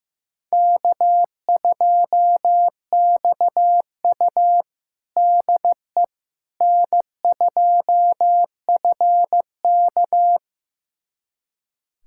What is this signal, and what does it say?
Morse code, heard as a keyed tone, about 16 wpm, 700 Hz. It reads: K2XU DE N2FK
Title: USA Callsign DE Callsign - 4 Characters - 15wpm